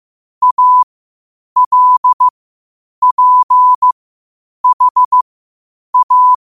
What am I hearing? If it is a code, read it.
Morse code, heard as a keyed tone, about 15 wpm, spaced as for 8 wpm, 1000 Hz. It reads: ALPHA
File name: qotdfiles / 8wpm